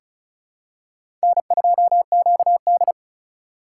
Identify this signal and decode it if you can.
Morse code, heard as a keyed tone, about 36 wpm, 700 Hz. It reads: N2QD
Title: USA Call Signs - 4 Characters - 4x Character Spacing - 35wpm